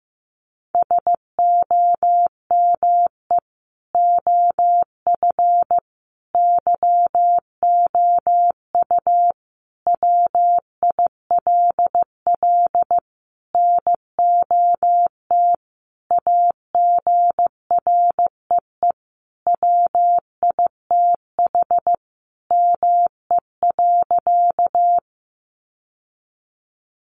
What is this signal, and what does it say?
Morse code, heard as a keyed tone, about 15 wpm, 700 Hz. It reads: SOME OF YOU WILL NOT AGREE WITH ME.
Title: Sentences from Top 1100 Words 15wpm